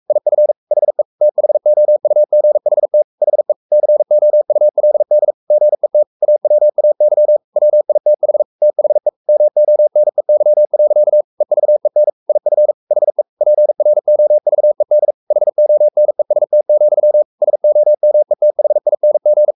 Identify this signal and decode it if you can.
Morse code, heard as a keyed tone, about 44 wpm, 600 Hz. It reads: IF HE THOUGHT HE COULD GET AWAY WITH THE MONEY. EVEN IF HE PROVED HONEST, SOMETHING
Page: Romanen Never af Ken Follet på engelsk.